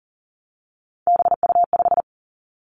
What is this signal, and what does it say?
Morse code, heard as a keyed tone, about 42 wpm, 700 Hz. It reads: BU5
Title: Three Challenging Letter-Number Combinations 40wpm